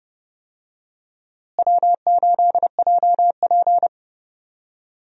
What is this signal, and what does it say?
Morse code, heard as a keyed tone, about 30 wpm, 700 Hz. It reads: W8JP
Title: USA Call Signs - 4 Characters - Repeated 3x - 30wpm